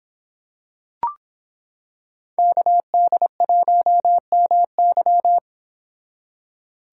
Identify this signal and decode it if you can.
Morse code, heard as a keyed tone, about 26 wpm, 700 Hz. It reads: KD1MY